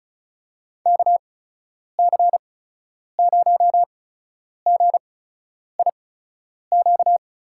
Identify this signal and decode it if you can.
Morse code, heard as a keyed tone, about 36 wpm, spaced as for 10 wpm, 700 Hz. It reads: KC0GIQ
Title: USA Call Signs - 8x Character Spacing - 35wpm